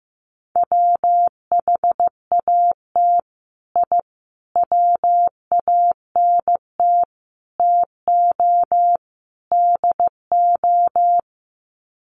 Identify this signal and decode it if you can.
Morse code, heard as a keyed tone, about 15 wpm, 700 Hz. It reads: WHAT I WANT TO DO
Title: Sets of 5 Words Encore 15wpm